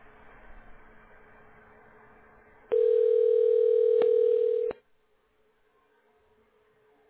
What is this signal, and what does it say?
Ringback tone (North America), 440 + 480 Hz.